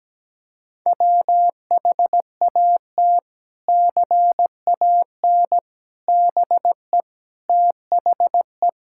Morse code, heard as a keyed tone, about 17 wpm, 700 Hz. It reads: WHAT CAN BE THE
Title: Sentences from Top 1000 Words - Review Part 6 - 17wpm